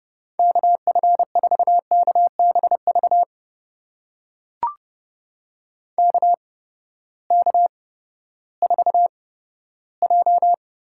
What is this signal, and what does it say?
Morse code, heard as a keyed tone, about 30 wpm, spaced as for 15 wpm, 700 Hz. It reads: KF4KBV KK4J
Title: USA Call Signs - 6 Characters - 8x Character Spacing - 30wpm